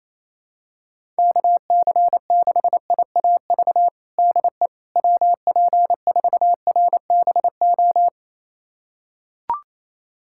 Morse code, heard as a keyed tone, about 28 wpm, 700 Hz. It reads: KC6IAV DE WP4RBO
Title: USA Callsign DE Callsign 28wpm